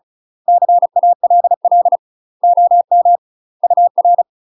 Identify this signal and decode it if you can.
Morse code, heard as a keyed tone, about 35 wpm, 700 Hz. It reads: CALL OM UR
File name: UK MASTER QSOs TRAINING / 35WPM / UK MASTER QSOs 0001